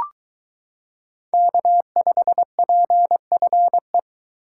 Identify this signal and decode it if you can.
Morse code, heard as a keyed tone, about 22 wpm, 700 Hz. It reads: K5PFE